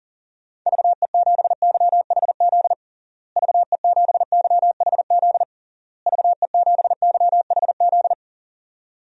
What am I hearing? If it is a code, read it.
Morse code, heard as a keyed tone, about 40 wpm, 700 Hz. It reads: VE7YHZ VE7YHZ VE7YHZ
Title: DXCC-100 Call Signs - 6 Characters - Repeated 3x - 40wpm